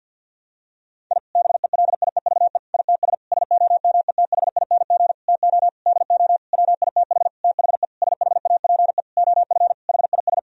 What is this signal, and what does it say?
Morse code, heard as a keyed tone, about 50 wpm, 700 Hz. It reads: I BELIEVE ITS SOMETHING TO DO WITH THE SHAPE OF HIS
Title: Sentences from Top 1000 Words - Review Part 3 - 50wpm